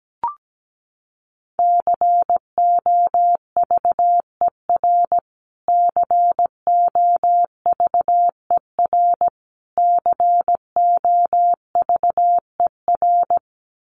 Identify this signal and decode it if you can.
Morse code, heard as a keyed tone, about 17 wpm, 700 Hz. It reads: COVER COVER COVER
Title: Top 500 Words Repeated 17wpm